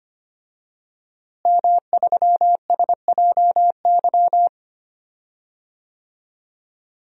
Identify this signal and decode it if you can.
Morse code, heard as a keyed tone, about 25 wpm, 700 Hz. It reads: M3SJY
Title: DXCC-100 Call Signs - 5 Characters - Repeated 3x - 25wpm